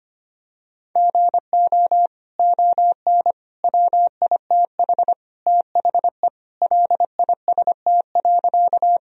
Morse code, heard as a keyed tone, about 25 wpm, 700 Hz. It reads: GO ON WITH THE LIST.
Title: Sentences from Top 1000 Words - Review Part 2 - 25wpm